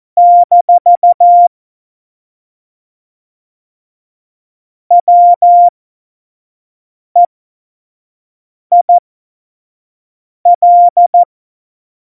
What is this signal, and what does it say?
Morse code, heard as a keyed tone, about 14 wpm, spaced as for 5 wpm, 700 Hz. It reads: - WEIL
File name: Morsekurs / Witze / Witz 17. September 2019, 25 F70